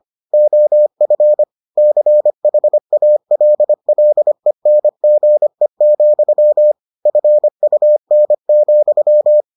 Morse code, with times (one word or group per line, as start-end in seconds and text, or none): 0.3-1.4 OF
1.8-6.7 CHALLENGE,
7.1-9.4 FUN,